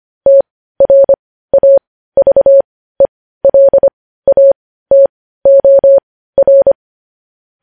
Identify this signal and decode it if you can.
Morse code, heard as a keyed tone, about 25 wpm, spaced as for 15 wpm, 550 Hz. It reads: TRAVELATOR
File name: podcast / News headlines-1x-25x15